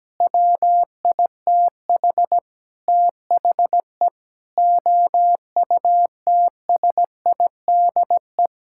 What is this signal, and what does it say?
Morse code, heard as a keyed tone, about 17 wpm, 700 Hz. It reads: WITH THE OUTSIDE
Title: Sentences from Top 1000 Words - Review Part 6 - 17wpm